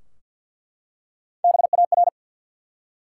Morse code, heard as a keyed tone, about 48 wpm, 700 Hz. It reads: BAR
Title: Three Letter Words - 50wpm